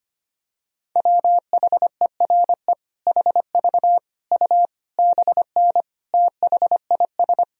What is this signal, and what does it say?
Morse code, heard as a keyed tone, about 25 wpm, 700 Hz. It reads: WHERE HV U BN THIS
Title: Sentences from Top 300 Words and Abbreviations 25wpm